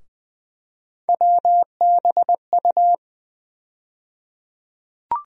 Morse code, heard as a keyed tone, about 20 wpm, 700 Hz. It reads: WBU
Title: USA Call Sign Postfixes - 3 Letters 20wpm